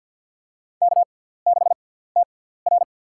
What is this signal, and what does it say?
Morse code, heard as a keyed tone, about 50 wpm, spaced as for 17 wpm, 700 Hz. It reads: K6TR
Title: USA Call Signs - 4 Characters - 6x Character Spacing - 50wpm